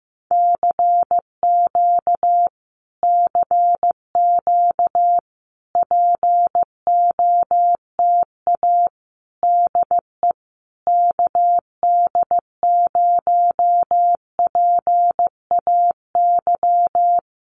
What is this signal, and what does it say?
Morse code, heard as a keyed tone, about 15 wpm, 700 Hz. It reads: CQ CQ POTA DE KD0PAY
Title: Common QSO Phrases Encore 15wpm